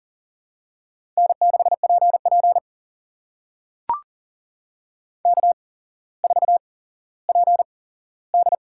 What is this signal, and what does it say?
Morse code, heard as a keyed tone, about 40 wpm, spaced as for 17 wpm, 700 Hz. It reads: N6PP K4PD
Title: USA Call Signs - 4 Characters - 8x Character Spacing - 40wpm